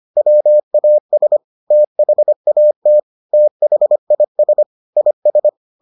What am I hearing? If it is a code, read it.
Morse code, heard as a keyed tone, about 25 wpm, 600 Hz. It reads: WAS THAT THIS IS